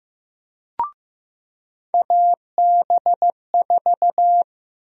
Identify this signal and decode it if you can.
Morse code, heard as a keyed tone, about 15 wpm, 700 Hz. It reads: AB4
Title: USA Call Sign Prefixes - 15wpm